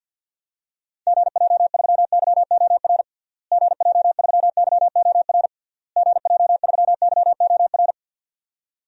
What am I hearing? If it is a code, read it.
Morse code, heard as a keyed tone, about 52 wpm, 700 Hz. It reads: GJ3YOR GJ3YOR GJ3YOR
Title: DXCC-100 Call Signs - 6 Characters - Repeated 3x - 50wpm